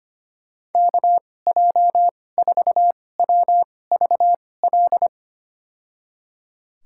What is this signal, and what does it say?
Morse code, heard as a keyed tone, about 25 wpm, spaced as for 18 wpm, 700 Hz. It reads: KJ4WVL
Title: USA Call Signs - 6 Characters - 2x Character Spacing - 25wpm